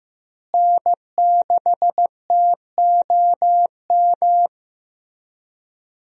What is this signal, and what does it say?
Morse code, heard as a keyed tone, about 15 wpm, 700 Hz. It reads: N6TOM
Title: USA Call Signs - 5 Characters - Repeated 2x - 15wpm